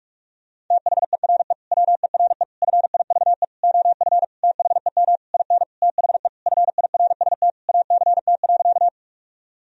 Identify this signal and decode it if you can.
Morse code, heard as a keyed tone, about 46 wpm, 700 Hz. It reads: THERE WERE FIVE OF THEM IN THE FIRST ACT.
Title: Sentences from Top 1000 Words - Review Part 2 - 45wpm